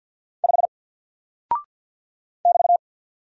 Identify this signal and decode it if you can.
Morse code, heard as a keyed tone, about 48 wpm, 700 Hz. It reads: 5 <BT>
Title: Single Character (Letter:Number:Punctuation:Prosign) Alt 50wpm